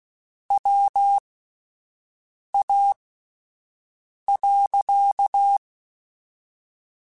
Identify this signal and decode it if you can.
Morse code, heard as a keyed tone, about 16 wpm, 800 Hz. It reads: W A .